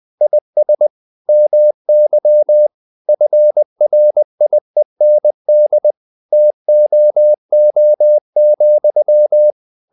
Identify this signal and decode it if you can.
Morse code, heard as a keyed tone, about 20 wpm, 600 Hz. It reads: IS MY FRIEND TOO,
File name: the old man and the sea part 2 20wpm